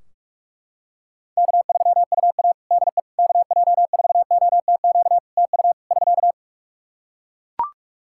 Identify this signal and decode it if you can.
Morse code, heard as a keyed tone, about 44 wpm, 700 Hz. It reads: K3UA DE KJ4OTQ TU <SK>
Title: Practice QSO - Exchange 4 - Ending - 45wpm